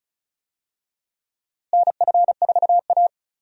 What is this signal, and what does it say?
Morse code, heard as a keyed tone, about 34 wpm, 700 Hz. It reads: NF4A
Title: USA Call Signs - 4 Characters - Repeated 2x - 35wpm